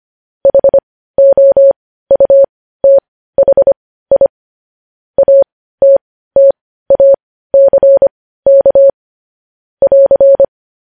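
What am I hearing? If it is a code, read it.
Morse code, heard as a keyed tone, about 25 wpm, spaced as for 15 wpm, 550 Hz. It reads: HOUTHI ATTACK <AR>
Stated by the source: Morse code practice of news headlines at 15 wpm.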